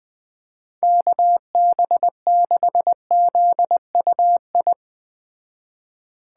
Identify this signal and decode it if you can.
Morse code, heard as a keyed tone, about 21 wpm, 700 Hz. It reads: KB6ZUI